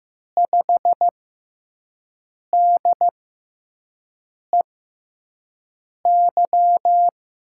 Morse code, heard as a keyed tone, about 15 wpm, spaced as for 5 wpm, 700 Hz. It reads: 5DEY
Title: USA Call Signs - 6 Characters - 6x Character Spacing - 15wpm